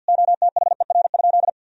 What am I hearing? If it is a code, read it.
Morse code, heard as a keyed tone, about 50 wpm, 700 Hz. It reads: OTHER?
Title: Sentences from Top 200 Words and Abbreviations 50wpm